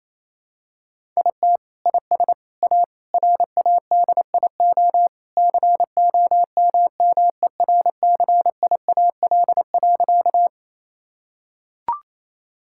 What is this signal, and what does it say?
Morse code, heard as a keyed tone, about 28 wpm, 700 Hz. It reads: IT IS A RADIO COMMERCIAL.
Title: Sentences from Top 1000 Words - Review Part 5 - 28wpm